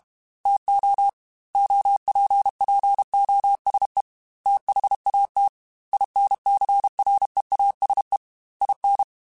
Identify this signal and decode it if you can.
Morse code, heard as a keyed tone, about 32 wpm, 800 Hz. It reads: TO OPPOSE THAT INCREASE IN